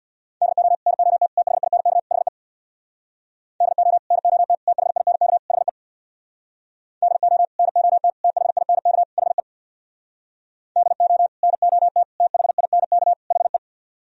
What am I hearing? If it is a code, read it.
Morse code, heard as a keyed tone, about 64 wpm, 700 Hz. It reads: DO NOT THINK HE DO NOT THINK HE DO NOT THINK HE DO NOT THINK HE